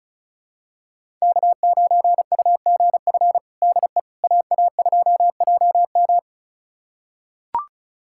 Morse code, heard as a keyed tone, about 36 wpm, 700 Hz. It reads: K9UGF DE AA2JM
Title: USA Callsign DE Callsign - 5 Characters - 35wpm